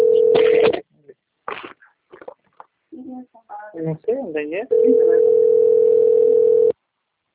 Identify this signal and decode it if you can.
Ringback tone (North America), 440 + 480 Hz, with speech.
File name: out- - - - - .52862